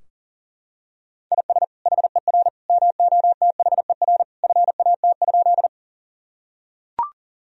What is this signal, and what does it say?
Morse code, heard as a keyed tone, about 40 wpm, 700 Hz. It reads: IS HER MOTHER FAT?
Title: Sentences from Top 1000 Words - Review Part 4 - 40wpm